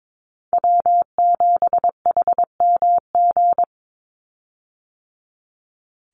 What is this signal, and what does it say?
Morse code, heard as a keyed tone, about 22 wpm, 700 Hz. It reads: W7HMG
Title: USA Call Signs 22wpm